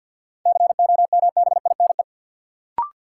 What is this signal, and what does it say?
Morse code, heard as a keyed tone, about 53 wpm, 700 Hz. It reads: COMBINE
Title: Sentences from Top 1600 Words - New Word Drill Alt - 50wpm